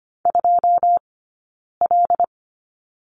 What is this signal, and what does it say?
Morse code, heard as a keyed tone, about 25 wpm, 700 Hz. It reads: 2 L
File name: Lesson 15d - Number 5 - Two Characters 25wpm